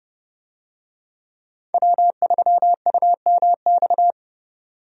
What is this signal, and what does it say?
Morse code, heard as a keyed tone, about 30 wpm, 700 Hz. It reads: W3UMX
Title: USA Call Signs - 5 Characters - 30wpm